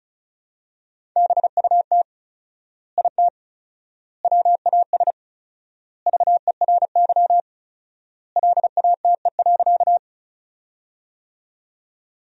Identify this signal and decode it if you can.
Morse code, heard as a keyed tone, about 36 wpm, 700 Hz. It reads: BUT IT WAS VERY LATE.
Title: 4x Word-Spacing - Sentences from Top 300 Words - 35wpm